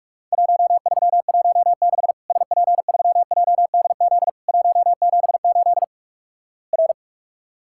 Morse code, heard as a keyed tone, about 47 wpm, 700 Hz. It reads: 1316 SP3JDZ 178 R
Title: Worked All Europe DX Contest - QTC Exchange - 45wpm